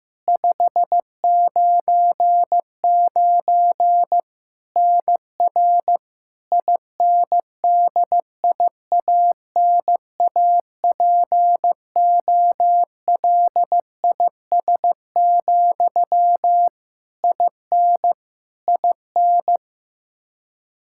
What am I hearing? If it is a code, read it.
Morse code, heard as a keyed tone, about 15 wpm, 700 Hz. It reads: 599 NR INDIANAPOLIS, IN IN
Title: Practice QSO - Exchange 1 - Beginning - 15wpm